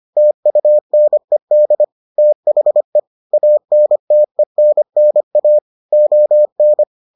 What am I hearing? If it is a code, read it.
Morse code, heard as a keyed tone, about 25 wpm, 600 Hz. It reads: TUNED THE ANTENNA ON